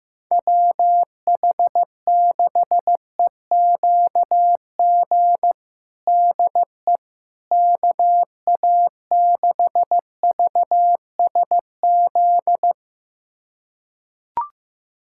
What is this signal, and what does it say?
Morse code, heard as a keyed tone, about 16 wpm, 700 Hz. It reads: WH6EQG DE KA6VSZ
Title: USA Callsign DE Callsign - 6 Characters - 15wpm